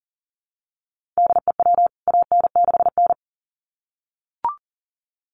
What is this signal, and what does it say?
Morse code, heard as a keyed tone, about 40 wpm, 700 Hz. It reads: DE WAN6N
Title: CWT Exchange 40wpm